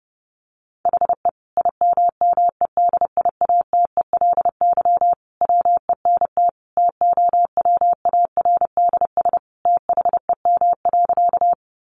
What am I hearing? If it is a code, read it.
Morse code, heard as a keyed tone, about 30 wpm, 700 Hz. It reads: HE IMMEDIATELY WENT TOWARDS THEM.
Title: Sentences from Top 1100 Words 30wpm